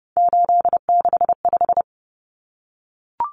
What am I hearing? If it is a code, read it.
Morse code, heard as a keyed tone, about 30 wpm, 700 Hz. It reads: 865
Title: Common Cut Numbers - 3-Digit 30wpm